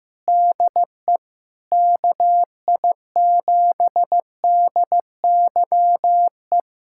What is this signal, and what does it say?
Morse code, heard as a keyed tone, about 15 wpm, 700 Hz. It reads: DE KI7DYE
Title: USA Callsign DE Callsign 15wpm